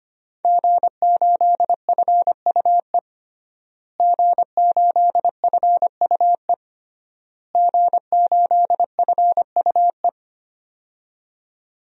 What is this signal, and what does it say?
Morse code, heard as a keyed tone, about 25 wpm, 700 Hz. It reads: G8FUE G8FUE G8FUE
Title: DXCC-100 Call Signs - 5 Characters - Repeated 3x - 25wpm